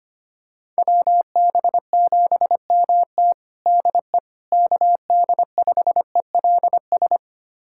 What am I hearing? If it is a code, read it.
Morse code, heard as a keyed tone, about 25 wpm, 700 Hz. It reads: WB7MT DE KD5ELS
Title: USA Callsign DE Callsign 25wpm